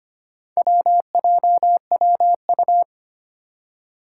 Morse code, heard as a keyed tone, about 25 wpm, 700 Hz. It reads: WJWU
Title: Four Challenging Letter-Number Combinations 25wpm